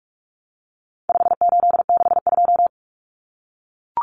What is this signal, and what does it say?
Morse code, heard as a keyed tone, about 44 wpm, 700 Hz. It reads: 5862
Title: Common Cut Numbers - 4-Digit 45wpm